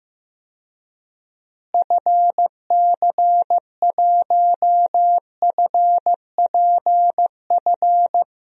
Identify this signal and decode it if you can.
Morse code, heard as a keyed tone, about 15 wpm, 700 Hz. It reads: FC1FPF
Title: DXCC-100 Call Signs - 6 Characters - Repeated 3x - 15wpm